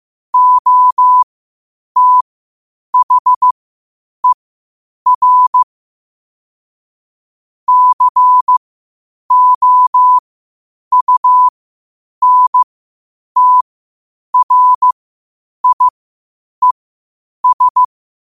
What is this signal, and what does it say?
Morse code, heard as a keyed tone, about 15 wpm, spaced as for 8 wpm, 1000 Hz. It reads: OTHER COUNTRIES
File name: qotdfiles / 8wpm